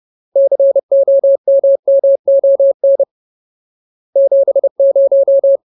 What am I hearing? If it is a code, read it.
Morse code, heard as a keyed tone, about 30 wpm, 550 Hz. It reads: COMMON 70
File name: Ultimate Word List / 30WPM Ultimate